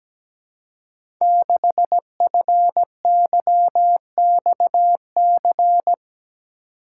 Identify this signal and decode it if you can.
Morse code, heard as a keyed tone, about 17 wpm, 700 Hz. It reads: 6FYXC
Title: Code Groups - 5 Letter:Numbers - 17wpm